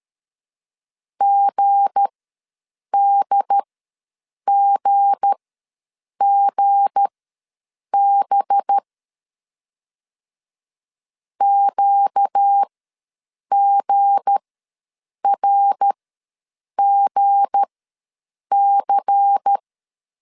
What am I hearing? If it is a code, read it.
Morse code, heard as a keyed tone, about 13 wpm, spaced as for 7 wpm, 800 Hz. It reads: GDGGB QGRGC
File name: LEZIONI / 7 GF9Z / 7 G ws 12wpm 3140